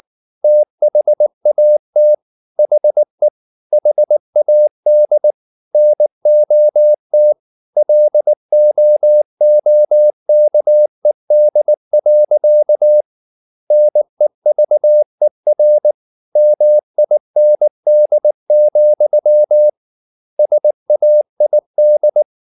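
Morse code, heard as a keyed tone, about 19 wpm, 600 Hz. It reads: THAT HE HAD NOT LOOKED. NEVER MIND, SAID
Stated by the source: Morse code at 19 wpm from CW med Gnister, reading Ken Follett's novel Never in English.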